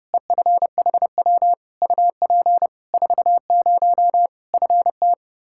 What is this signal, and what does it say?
Morse code, heard as a keyed tone, about 32 wpm, 700 Hz. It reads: EFHW UP 40 FT
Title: Practice QSO - Exchange 2 - Info - 30wpm